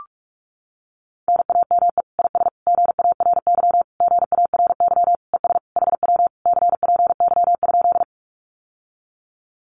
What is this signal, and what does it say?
Morse code, heard as a keyed tone, about 44 wpm, 700 Hz. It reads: NAME IS GARY GARY ES HW CPY?
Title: Common QSO Phrases Encore 45wpm